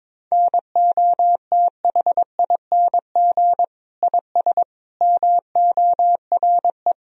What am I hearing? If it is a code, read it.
Morse code, heard as a keyed tone, about 22 wpm, 700 Hz. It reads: NOTHING IS MORE
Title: Sentences from Top 1000 Words - Review Part 4 - 22wpm